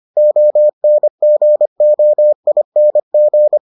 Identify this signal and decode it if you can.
Morse code, heard as a keyed tone, about 25 wpm, 600 Hz. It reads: ONGOING